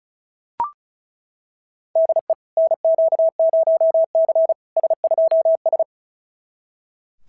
Morse code, heard as a keyed tone, about 34 wpm, 650 Hz. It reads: DE NQ0C S2S
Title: SOTA QSO Practice 35wpm